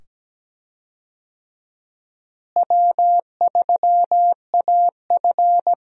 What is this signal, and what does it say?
Morse code, heard as a keyed tone, about 17 wpm, 700 Hz. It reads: W3AF
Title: USA Call Signs - 4 Characters - Repeated 2x - 17wpm